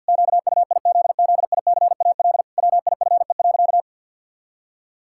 Morse code, heard as a keyed tone, about 50 wpm, 700 Hz. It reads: QUIZZICAL WIFE.
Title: Pangram Challenge v2 - 50wpm